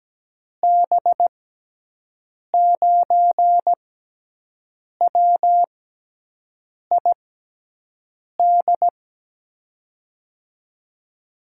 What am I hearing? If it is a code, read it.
Morse code, heard as a keyed tone, about 17 wpm, spaced as for 6 wpm, 700 Hz. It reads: B9WID
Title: USA Call Signs - 6x Character Spacing - 17wpm